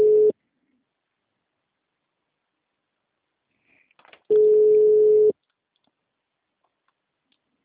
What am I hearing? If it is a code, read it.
Ringback tone (Europe, 425 Hz).